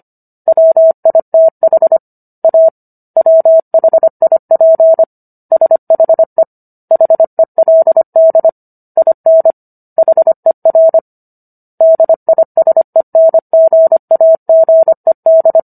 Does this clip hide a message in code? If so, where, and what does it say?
Yes: Morse code from 0.5 to 15.6 s, heard as a keyed tone, about 25 wpm, 650 Hz. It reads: WITH A WHIP SHE HELD IN HER DISENGAGED